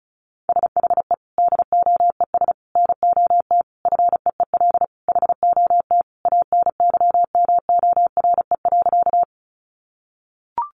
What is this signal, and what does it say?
Morse code, heard as a keyed tone, about 34 wpm, 700 Hz. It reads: SHE DOES NOT FEEL HOT ANYMORE.
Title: Additional Easy Phrases 35wpm